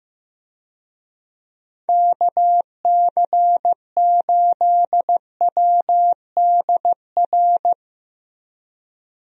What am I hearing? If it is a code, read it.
Morse code, heard as a keyed tone, about 15 wpm, 700 Hz. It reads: KC8WDR
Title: USA Call Signs - 6 Characters - Repeated 2x - 15wpm